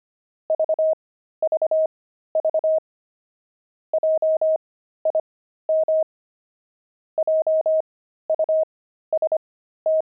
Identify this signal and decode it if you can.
Morse code, heard as a keyed tone, about 25 wpm, spaced as for 13 wpm, 630 Hz. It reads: VVV JIM JUST
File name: pangram-fs-13-cs-25-quit